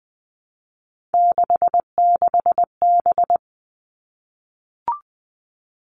Morse code, heard as a keyed tone, about 20 wpm, 700 Hz. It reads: 66B
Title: All Cut Numbers - 3-Digit 20wpm